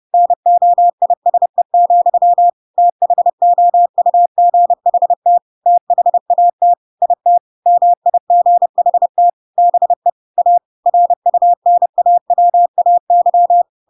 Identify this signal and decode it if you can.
Morse code, heard as a keyed tone, about 30 wpm, 700 Hz. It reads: NOISE, THOUGHT THAT IT MIGHT BE A RUNAWAY
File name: Pinocchio / Pinocchio ch3 30 30 ik1hge